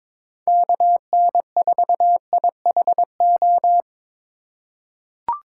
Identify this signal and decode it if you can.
Morse code, heard as a keyed tone, about 22 wpm, 700 Hz. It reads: KN4IHO
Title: USA Call Signs - 6 Characters - 4x Character Spacing - 22wpm